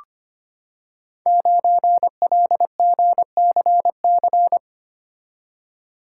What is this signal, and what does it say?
Morse code, heard as a keyed tone, about 25 wpm, 700 Hz. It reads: 9LGCC
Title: Code Groups - 5 Letter:Numbers - 25wpm